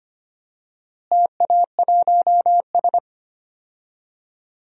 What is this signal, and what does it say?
Morse code, heard as a keyed tone, about 25 wpm, 700 Hz. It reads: TA1S
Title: DXCC-100 Call Signs - 4 Characters - Repeated 3x - 25wpm